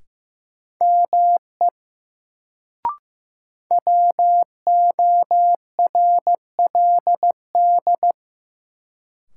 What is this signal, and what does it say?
Morse code, heard as a keyed tone, about 15 wpm, 700 Hz. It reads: ME WORLD
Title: Sets of 2 Words - New Word Drill - 15wpm